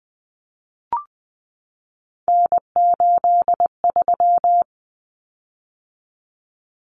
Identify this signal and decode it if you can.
Morse code, heard as a keyed tone, about 20 wpm, 700 Hz. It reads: N83
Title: Common Cut Numbers - 3-Digit 20wpm